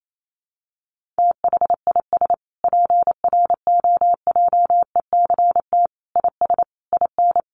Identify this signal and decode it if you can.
Morse code, heard as a keyed tone, about 28 wpm, 700 Hz. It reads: THIS PROJECT IS IN
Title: Sentences from Top 1400 Words 28wpm